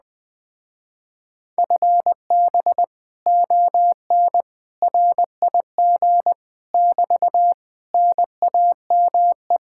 Morse code, heard as a keyed tone, about 20 wpm, 700 Hz. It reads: FB ON RIG <BT> NAME